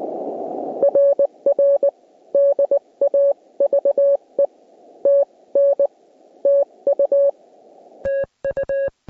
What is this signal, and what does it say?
Morse code, heard as a keyed tone, about 21 wpm, spaced as for 18 wpm, 560 Hz. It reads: RR DAVE TN TU TU